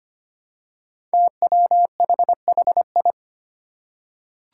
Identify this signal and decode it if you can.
Morse code, heard as a keyed tone, about 25 wpm, 700 Hz. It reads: TWHHI
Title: Five Challenging Letter-Number Combinations 25wpm